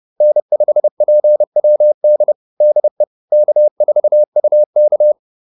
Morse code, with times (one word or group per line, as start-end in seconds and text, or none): 0.2-2.3 N5PWD
2.6-3.0 DE
3.3-5.1 K4UK